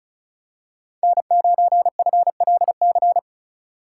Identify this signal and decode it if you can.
Morse code, heard as a keyed tone, about 36 wpm, 700 Hz. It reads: N9FLC
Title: USA Call Signs - 2x Character Spacing - 35wpm